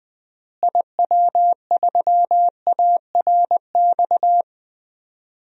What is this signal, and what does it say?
Morse code, heard as a keyed tone, about 20 wpm, 700 Hz. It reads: IW3ARX
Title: DXCC-100 Call Signs - 6 Characters - Repeated 3x - 20wpm